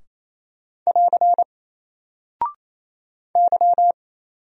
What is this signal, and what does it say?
Morse code, heard as a keyed tone, about 27 wpm, 700 Hz. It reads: <AR> Y
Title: Single Character (Letter:Number:Punctuation:Prosign) Alt 28wpm